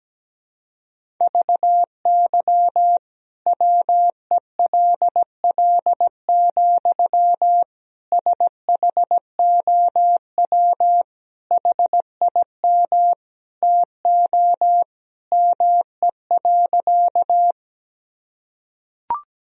Morse code, heard as a keyed tone, about 17 wpm, 700 Hz. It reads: VY WELL, SHOW HIM TO ME.
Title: Sentences from Top 200 Words and Abbreviations 17wpm